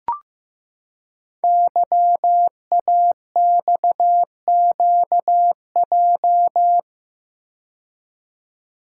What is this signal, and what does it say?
Morse code, heard as a keyed tone, about 15 wpm, 690 Hz. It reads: YAXQJ
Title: Code Groups - 5 Letters - 15wpm